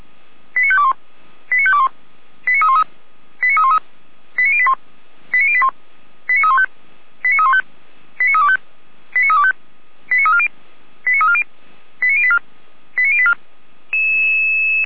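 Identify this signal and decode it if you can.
Five-tone selective call (ZVEI), heard as quick runs of five tones, tones of about 80 ms. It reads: 89521 89521 89313 89313 89081 89081 89315 89315 89325 89325 89340 89340 89084 89084
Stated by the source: Received on SDR as ZVEI five-tone selective call, address 89514.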